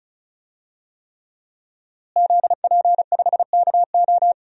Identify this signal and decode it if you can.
Morse code, heard as a keyed tone, about 36 wpm, 700 Hz. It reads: ZP5KO
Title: DXCC-100 Call Signs - 5 Characters - Repeated 3x - 35wpm